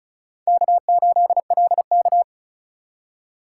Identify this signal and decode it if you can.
Morse code, heard as a keyed tone, about 34 wpm, 700 Hz. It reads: K8LK
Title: USA Call Signs - 2x Character Spacing - 35wpm